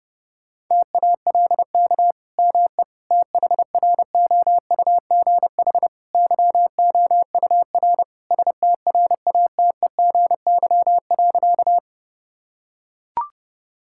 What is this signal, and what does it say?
Morse code, heard as a keyed tone, about 30 wpm, 700 Hz. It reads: TALK ME THROUGH YOUR STRATEGY.
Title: Sentences from Top 1000 Words - Review Part 5 - 30wpm